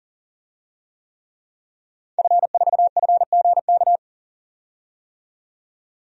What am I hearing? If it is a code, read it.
Morse code, heard as a keyed tone, about 40 wpm, 700 Hz. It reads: F4FGK
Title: DXCC-100 Call Signs - 5 Characters - 40wpm